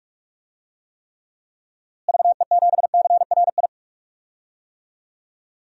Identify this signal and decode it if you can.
Morse code, heard as a keyed tone, about 46 wpm, 700 Hz. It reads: VE7CRI
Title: DXCC-100 Call Signs - 6 Characters - Repeated 3x - 45wpm